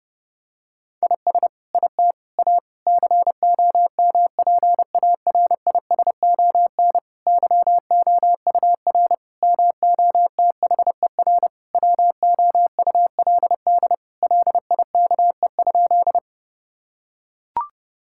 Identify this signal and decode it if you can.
Morse code, heard as a keyed tone, about 32 wpm, 700 Hz. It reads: IS IT A COMPARISON YOUR MOTHER WOULD LIKE?
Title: Sentences from Top 1600 Words 30wpm